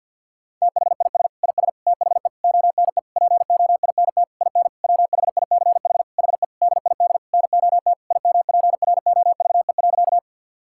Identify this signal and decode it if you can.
Morse code, heard as a keyed tone, about 50 wpm, 700 Hz. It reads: THIS IS THE ONE POINT IN WHICH HE DID NOT IMPROVE.
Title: Sentences from Top 1000 Words - Review Part 5 - 50wpm